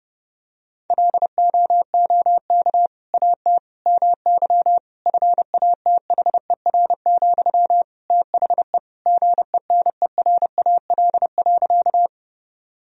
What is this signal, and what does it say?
Morse code, heard as a keyed tone, about 30 wpm, 700 Hz. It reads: LOOK AT MY FATHER, THE GENERAL.
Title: Sentences from Top 1000 Words - Review Part 3 - 30wpm